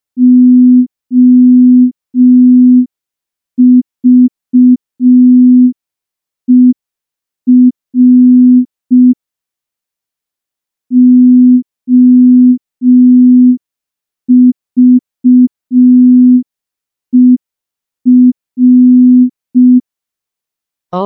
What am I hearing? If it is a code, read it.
Morse code, heard as a keyed tone, about 5 wpm, 250 Hz. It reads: OVER OVER
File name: morse / morse-training-05WPM